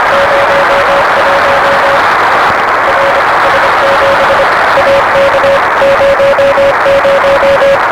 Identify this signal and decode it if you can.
Morse code, heard as a keyed tone, about 25 wpm, 560 Hz, with noise. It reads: 01 RS7 AK 00
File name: RS7 Beacon 16 Feb 1985